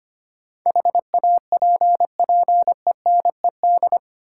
Morse code, heard as a keyed tone, about 25 wpm, 700 Hz. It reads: HAPPENED
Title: Sentences from Top 1000 Words - Review Part 2 - New Word Drill - 25wpm